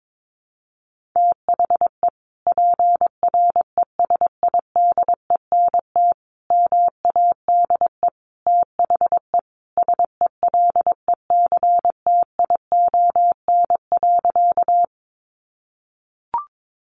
Morse code, heard as a keyed tone, about 22 wpm, 700 Hz. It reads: THE PRESIDENT MADE THE SELECTION.
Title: Sentences from Top 1400 Words 22wpm